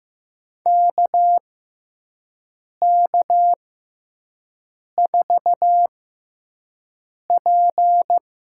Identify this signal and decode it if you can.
Morse code, heard as a keyed tone, about 15 wpm, spaced as for 5 wpm, 700 Hz. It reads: KK4P
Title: USA Call Signs - 6 Characters - 6x Character Spacing - 15wpm